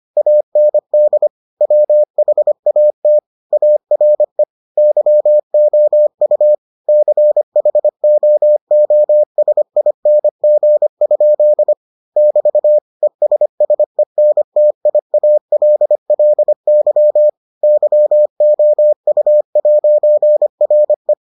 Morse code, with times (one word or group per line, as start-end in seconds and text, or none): 0.2-1.3 AND
1.6-3.2 WHAT
3.5-4.4 ARE
4.8-6.5 YOU
6.9-11.7 CHOOSING?
12.2-12.8 <BT>
13.0-17.3 ESSENTIALLY
17.6-21.1 YOU'RE